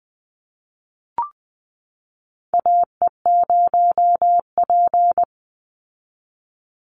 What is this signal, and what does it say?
Morse code, heard as a keyed tone, about 20 wpm, 700 Hz. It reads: AE0P
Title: Call Signs Repeated 3x - 20wpm